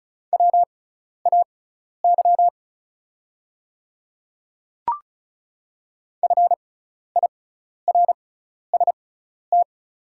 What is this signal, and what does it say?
Morse code, heard as a keyed tone, about 36 wpm, spaced as for 12 wpm, 690 Hz. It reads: WAY FIRST
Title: 6x Character Spacing - Top 100 Words - 35wpm